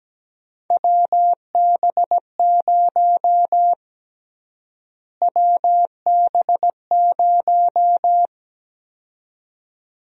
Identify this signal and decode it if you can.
Morse code, heard as a keyed tone, about 17 wpm, 700 Hz. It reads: WB0 WB0
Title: USA Call Sign Prefixes - Repeated 2x - 17wpm